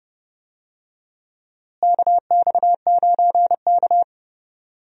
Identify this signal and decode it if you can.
Morse code, heard as a keyed tone, about 30 wpm, 700 Hz. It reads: KX9K
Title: USA Call Signs - 4 Characters - Repeated 2x - 30wpm